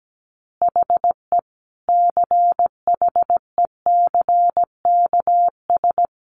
Morse code, heard as a keyed tone, about 17 wpm, 700 Hz. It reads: HE CHECKS
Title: Easy Phrases 17wpm